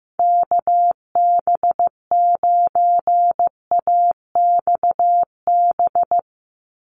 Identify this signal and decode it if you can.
Morse code, heard as a keyed tone, about 15 wpm, 700 Hz. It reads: KB9AXB
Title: USA Call Signs 15wpm